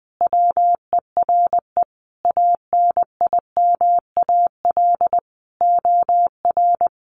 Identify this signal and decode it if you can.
Morse code, heard as a keyed tone, about 20 wpm, 700 Hz. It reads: WERE ANIMAL OR
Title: Sentences from Top 1300 Words 20wpm